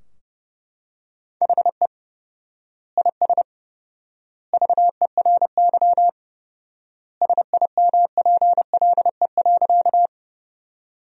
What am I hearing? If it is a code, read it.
Morse code, heard as a keyed tone, about 32 wpm, 700 Hz. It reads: HE IS VERY SIMPLE.
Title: 4x Word-Spacing - Sentences from Top 400 Words - 30wpm